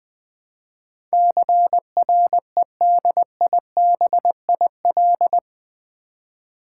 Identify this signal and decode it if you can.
Morse code, heard as a keyed tone, about 20 wpm, 700 Hz. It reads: CREDIBIL
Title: Build A Word - 4 Letters At A Time - 11-Letter Words - 20wpm